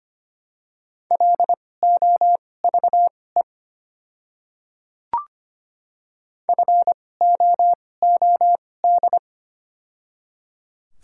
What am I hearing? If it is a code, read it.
Morse code, heard as a keyed tone, about 25 wpm, spaced as for 18 wpm, 700 Hz. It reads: LOVE FOOD
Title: 2x Character Spacing - Top 500 Words - 25wpm